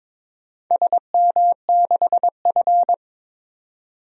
Morse code, heard as a keyed tone, about 22 wpm, 700 Hz. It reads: SM6F
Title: DXCC-100 Call Signs - 4 Characters - 22wpm